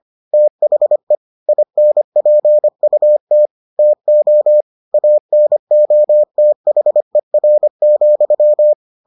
Morse code, heard as a keyed tone, about 25 wpm, 600 Hz. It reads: THE INPUT TO ANOTHER,